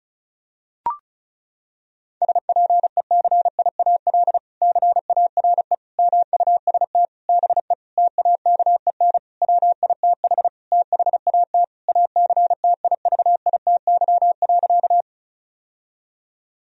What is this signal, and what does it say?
Morse code, heard as a keyed tone, about 34 wpm, 700 Hz. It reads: SPECIAL CARE MUST BE TAKEN WITH THAT ACTIVITY.
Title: Sentences from Top 1000 Words - Review Part 4 - 35wpm